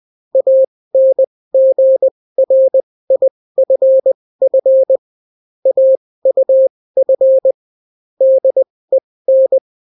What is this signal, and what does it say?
Morse code, heard as a keyed tone, about 20 wpm, spaced as for 16 wpm, 530 Hz. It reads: ANGRIFF AUF DEN